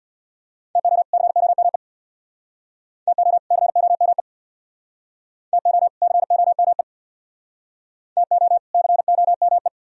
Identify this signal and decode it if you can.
Morse code, heard as a keyed tone, about 62 wpm, 700 Hz. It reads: TO COME TO COME TO COME TO COME
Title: Speed-Racing - Sets of 2 Words 50wpm